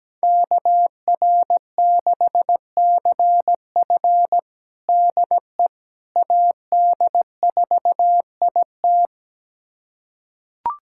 Morse code, heard as a keyed tone, about 17 wpm, 700 Hz. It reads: KR6CF DE AD4IT
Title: USA Callsign DE Callsign - 5 Characters - 17wpm